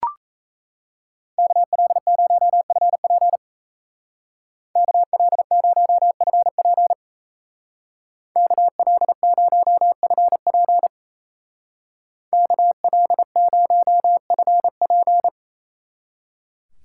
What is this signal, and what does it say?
Morse code, heard as a keyed tone, about 34 wpm, 700 Hz. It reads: KL0FP KL0FP KL0FP KL0FP
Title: Speed Racing - USA 2x2 Call Signs 28wpm